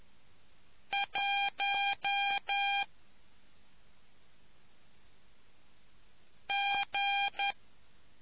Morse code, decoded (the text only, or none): 1 G